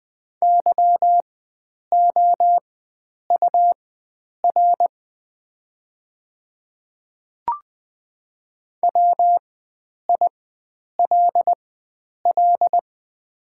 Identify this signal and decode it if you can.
Morse code, heard as a keyed tone, about 20 wpm, spaced as for 9 wpm, 700 Hz. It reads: YOUR WILL
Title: 4x Character Spacing - Top 100 Words - 20wpm